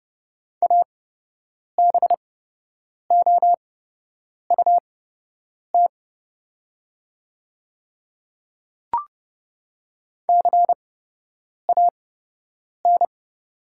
Morse code, heard as a keyed tone, about 30 wpm, spaced as for 8 wpm, 700 Hz. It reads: ABOUT CAN
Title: 8x Character Spacing - Top 100 Words - 30wpm